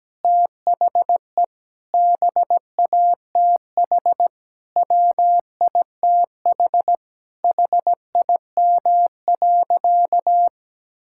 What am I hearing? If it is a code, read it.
Morse code, heard as a keyed tone, about 17 wpm, 700 Hz. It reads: THE BATH WITH HIM.
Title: Sentences from Top 1600 Words 17wpm